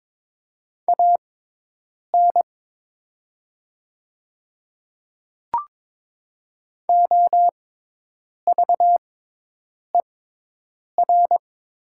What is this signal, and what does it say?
Morse code, heard as a keyed tone, about 22 wpm, spaced as for 8 wpm, 700 Hz. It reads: AN OVER
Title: 6x Character Spacing - Top 100 Words - 22wpm